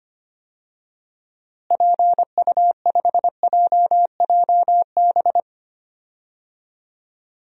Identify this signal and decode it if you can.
Morse code, heard as a keyed tone, about 25 wpm, 700 Hz. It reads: PU5JJB
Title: DXCC-100 Call Signs - 6 Characters - Repeated 3x - 25wpm